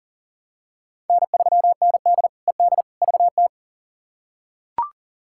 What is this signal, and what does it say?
Morse code, heard as a keyed tone, about 40 wpm, 700 Hz. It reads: N3ND ED VT
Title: K1USN SST QSO Practice - 40wpm